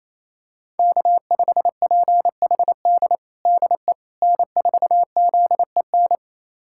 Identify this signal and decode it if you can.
Morse code, heard as a keyed tone, about 28 wpm, 700 Hz. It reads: K5PHD DE N4ZEN
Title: USA Callsign DE Callsign - 5 Characters - 28wpm